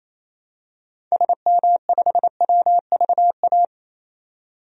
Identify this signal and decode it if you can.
Morse code, heard as a keyed tone, about 28 wpm, 700 Hz. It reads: SM5WVA
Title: DXCC-100 Call Signs - 6 Characters - 28wpm